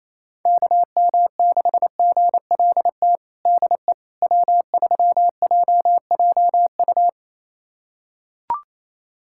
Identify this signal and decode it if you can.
Morse code, heard as a keyed tone, about 28 wpm, 700 Hz. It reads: KM6GLT DE W3JJU
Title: USA Callsign DE Callsign 28wpm